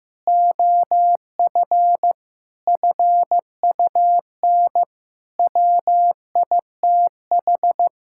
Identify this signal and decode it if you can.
Morse code, heard as a keyed tone, about 15 wpm, 700 Hz. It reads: OF FUN WITH
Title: Sentences from Top 1000 Words - Review Part 5 - 15wpm